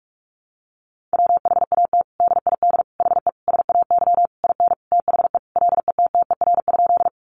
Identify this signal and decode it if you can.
Morse code, heard as a keyed tone, about 44 wpm, 700 Hz. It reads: WHAT DID HE SAY IN THE LETTER?
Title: Sentences from Top 300 Words 45wpm